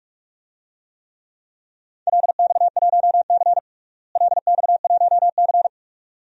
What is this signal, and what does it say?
Morse code, heard as a keyed tone, about 44 wpm, 700 Hz. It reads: LX1C LX1C
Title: DXCC-100 Call Signs - 4 Characters - Repeated 2x - 45wpm